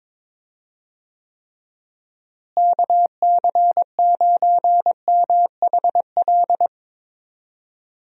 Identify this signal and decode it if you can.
Morse code, heard as a keyed tone, about 22 wpm, 700 Hz. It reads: KC9MHL
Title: USA Call Signs - 6 Characters - Repeated 2x - 22wpm